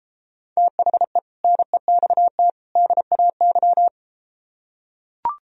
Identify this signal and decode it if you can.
Morse code, heard as a keyed tone, about 33 wpm, 700 Hz. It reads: THE NEXT DAY